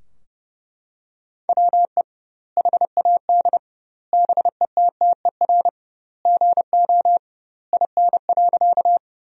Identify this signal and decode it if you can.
Morse code, heard as a keyed tone, about 30 wpm, 700 Hz. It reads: WE HAD BETTER GO IN.
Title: 2x Word-Spacing - Sentences from Top 400 Words - 30wpm